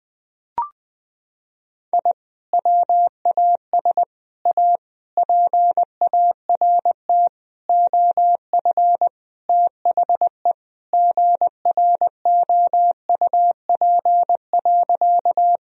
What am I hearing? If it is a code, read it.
Morse code, heard as a keyed tone, about 20 wpm, 700 Hz. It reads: I WAS A PART OF THE GROUP.
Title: Sentences from Top 1000 Words - Review Part 1 - 20wpm